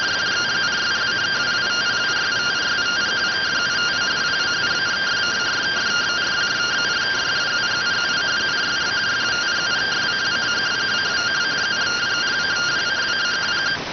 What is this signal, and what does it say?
Two-tone teleprinter signal, about 170 Hz shift.